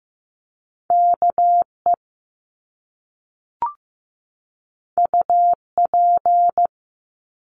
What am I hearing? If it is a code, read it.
Morse code, heard as a keyed tone, about 15 wpm, 700 Hz. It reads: KE UP
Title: Top Two Letter Combinations in Words 15wpm